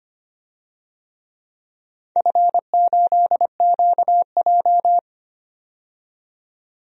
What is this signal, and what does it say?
Morse code, heard as a keyed tone, about 25 wpm, 700 Hz. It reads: F8QJ
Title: DXCC-100 Call Signs - 4 Characters - 25wpm